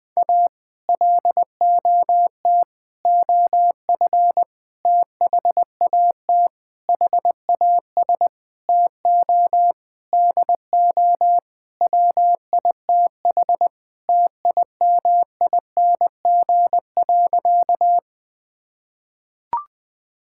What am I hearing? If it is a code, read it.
Morse code, heard as a keyed tone, about 21 wpm, 700 Hz. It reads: A LOT OF THAT HAS TO DO WITH TIMING.
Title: Sentences with 8-Word Phrases - 20wpm